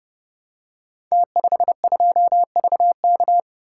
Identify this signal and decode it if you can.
Morse code, heard as a keyed tone, about 30 wpm, 700 Hz. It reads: T52VK
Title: Code Groups - 5 Letter:Numbers - 30wpm